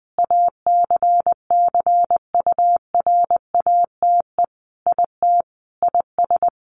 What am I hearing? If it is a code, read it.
Morse code, heard as a keyed tone, about 20 wpm, 700 Hz. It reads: ACCURATE IT IS
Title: Sentences from Top 1200 Words 20wpm